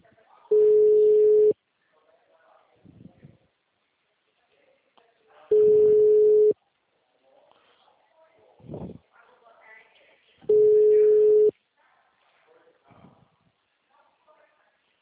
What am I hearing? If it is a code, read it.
Ringback tone (Europe, 425 Hz), with speech.
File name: out- -2033- - - .65121